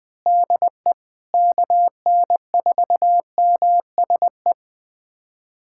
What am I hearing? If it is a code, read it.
Morse code, heard as a keyed tone, about 21 wpm, 700 Hz. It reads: DE KN4MSE
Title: USA Callsign DE Callsign 20wpm